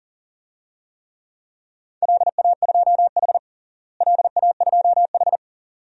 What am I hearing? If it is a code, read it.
Morse code, heard as a keyed tone, about 40 wpm, 700 Hz. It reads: LA2H LA2H
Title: DXCC-100 Call Signs - 4 Characters - Repeated 2x - 40wpm